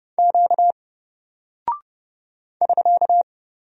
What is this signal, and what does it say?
Morse code, heard as a keyed tone, about 31 wpm, 700 Hz. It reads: Q <SK>
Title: Single Character (Letter:Number:Punctuation:Prosign) 30wpm